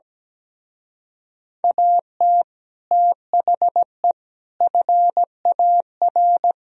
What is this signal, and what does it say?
Morse code, heard as a keyed tone, about 17 wpm, 700 Hz. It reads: AT THE FAR
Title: Speed-Racing - Sets of 5 Words 17wpm